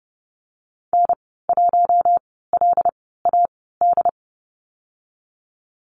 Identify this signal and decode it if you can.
Morse code, heard as a keyed tone, about 30 wpm, spaced as for 17 wpm, 700 Hz. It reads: N1LAD
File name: Lesson 14g - Number 2 - Call Signs 30wpm